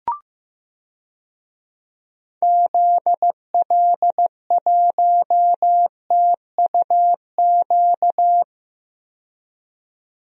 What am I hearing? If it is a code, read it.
Morse code, heard as a keyed tone, about 15 wpm, 700 Hz. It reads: ZL1TUQ
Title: DXCC-100 Call Signs - 6 Characters - 15wpm